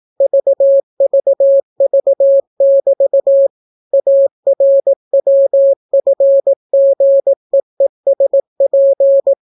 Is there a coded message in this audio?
Yes: Morse code from 0.2 to 9.3 s, heard as a keyed tone, about 18 wpm, 550 Hz. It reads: VVV<BT> ARWFGEESP